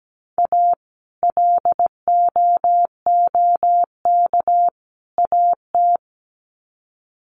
Morse code, heard as a keyed tone, about 17 wpm, 700 Hz. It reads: A LOOK AT
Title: Sets of 3 Words 17wpm